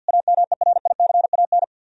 Morse code, heard as a keyed tone, about 62 wpm, 700 Hz. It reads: AMERICAN
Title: Speed-Racing - Sets of 2 Words 50wpm